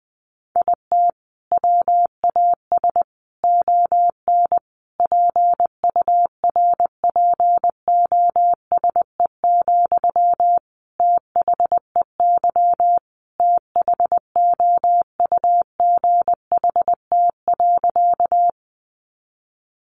Morse code, heard as a keyed tone, about 20 wpm, 700 Hz. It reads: IT WAS ON PURPOSE, THEY THOUGHT.
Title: Sentences from Top 800 Words 20wpm